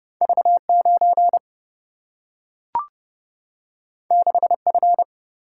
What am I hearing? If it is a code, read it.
Morse code, heard as a keyed tone, about 30 wpm, 700 Hz. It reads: V 9 6 F
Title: Single-Letter Single-Number 30wpm